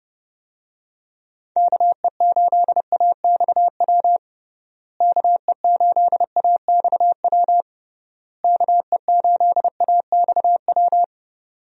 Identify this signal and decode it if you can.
Morse code, heard as a keyed tone, about 30 wpm, 700 Hz. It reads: KE8AXW KE8AXW KE8AXW
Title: USA Call Signs - 6 Characters - Repeated 3x - 30wpm